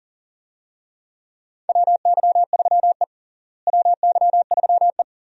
Morse code, heard as a keyed tone, about 40 wpm, 700 Hz. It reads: WY3E WY3E
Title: USA Call Signs - 4 Characters - Repeated 3x - 40wpm